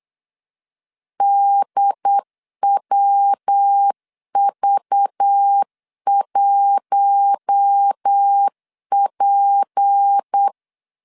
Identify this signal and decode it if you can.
Morse code, heard as a keyed tone, about 8 wpm, 800 Hz. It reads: DWV1P
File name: LEZIONI / 6 PKX0 / 6 PK 8wpm 3003